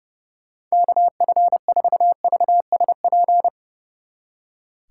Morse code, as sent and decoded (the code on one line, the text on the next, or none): -.- ..-. ....- ...- ... .--.
KF4VSP